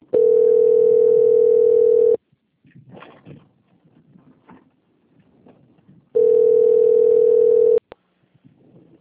Ringback tone (North America), 440 + 480 Hz.
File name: out- - - - - .366